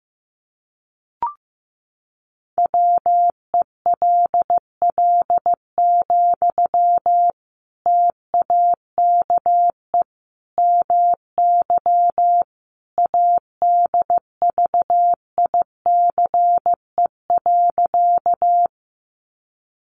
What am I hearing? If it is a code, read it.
Morse code, heard as a keyed tone, about 15 wpm, 700 Hz. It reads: WELL, TAKE MY ADVICE.
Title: Sentences from Top 1100 Words 15wpm